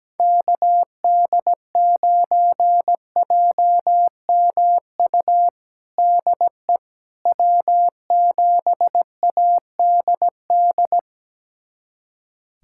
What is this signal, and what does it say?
Morse code, heard as a keyed tone, about 17 wpm, 700 Hz. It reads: KD9JMU DE W7ADD
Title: USA Callsign DE Callsign 17wpm